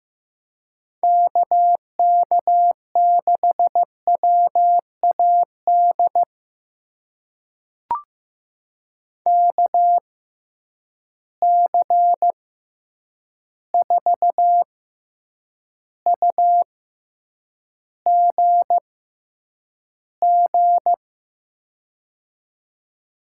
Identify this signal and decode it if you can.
Morse code, heard as a keyed tone, about 15 wpm, spaced as for 8 wpm, 700 Hz. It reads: KK6WAD KC4UGG
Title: USA Call Signs - 6 Characters - 6x Character Spacing - 15wpm